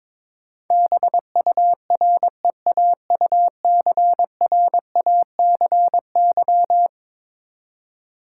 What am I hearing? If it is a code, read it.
Morse code, heard as a keyed tone, about 21 wpm, 700 Hz. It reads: BUREAUCRACY
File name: practice / 22wpm / Eleven Letter Words - 22wpm